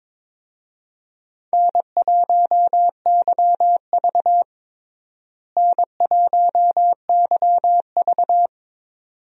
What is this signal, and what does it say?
Morse code, heard as a keyed tone, about 22 wpm, 700 Hz. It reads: N1YV N1YV
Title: USA Call Signs - 4 Characters - Repeated 2x - 22wpm